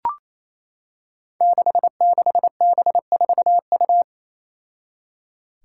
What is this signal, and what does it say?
Morse code, heard as a keyed tone, about 28 wpm, 700 Hz. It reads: 66B4U
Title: Five Challenging Letter-Number Combinations 28wpm